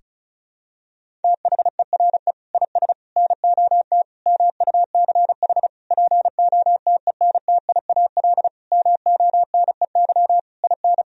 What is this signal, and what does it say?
Morse code, heard as a keyed tone, about 36 wpm, 700 Hz. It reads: THERE IS NOT MUCH POTENTIAL MONEY IN
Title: Sentences from Top 1000 Words - Review Part 4 - 35wpm